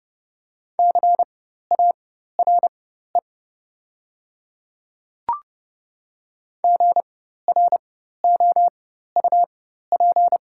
Morse code, heard as a keyed tone, about 30 wpm, spaced as for 14 wpm, 690 Hz. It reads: CARE GROUP
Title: 4x Character Spacing - Top 300 Words - 30wpm